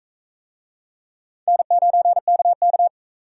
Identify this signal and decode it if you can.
Morse code, heard as a keyed tone, about 38 wpm, 700 Hz. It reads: N9KK